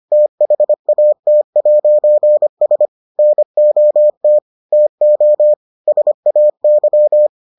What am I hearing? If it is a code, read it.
Morse code, heard as a keyed tone, about 25 wpm, 600 Hz. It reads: THAT'S NOT TO SAY